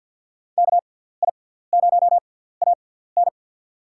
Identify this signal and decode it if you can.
Morse code, heard as a keyed tone, about 48 wpm, spaced as for 17 wpm, 700 Hz. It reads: KI0AN
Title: USA Call Signs - 5 Characters - 6x Character Spacing - 50wpm